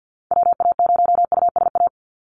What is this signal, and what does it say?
Morse code, heard as a keyed tone, about 50 wpm, 700 Hz. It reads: WA0USA
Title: Call Signs Repeated 3x - 50wpm